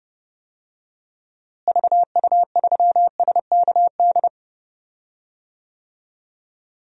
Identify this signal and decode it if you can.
Morse code, heard as a keyed tone, about 30 wpm, 700 Hz. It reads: VU3SKD
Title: DXCC-100 Call Signs - 6 Characters - Repeated 2x - 30wpm